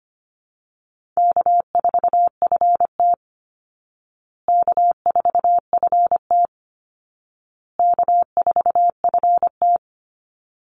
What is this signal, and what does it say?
Morse code, heard as a keyed tone, about 25 wpm, 700 Hz. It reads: K4FT K4FT K4FT
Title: Call Signs Repeated 2x - 25wpm